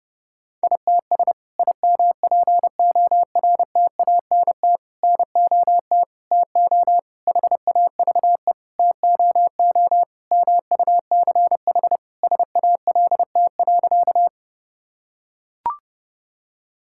Morse code, heard as a keyed tone, about 28 wpm, 700 Hz. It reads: ITS IMPORTANT NOT TO HAVE TOO MUCH SALT.
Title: Sentences from Top 1000 Words - Review Part 5 - 30wpm